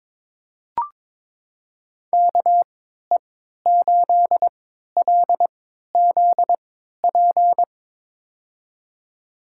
Morse code, heard as a keyed tone, about 22 wpm, spaced as for 12 wpm, 700 Hz. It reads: KE8LZP
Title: USA Call Signs - 6 Characters - 3x Character Spacing - 22wpm